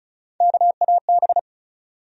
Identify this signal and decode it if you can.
Morse code, heard as a keyed tone, about 36 wpm, 700 Hz. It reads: KAB
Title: USA Call Sign Postfixes - 3 Letters 35wpm